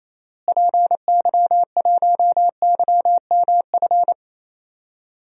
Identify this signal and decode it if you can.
Morse code, heard as a keyed tone, about 28 wpm, 700 Hz. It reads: PY1YMF
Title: DXCC-100 Call Signs - 6 Characters - Repeated 2x - 28wpm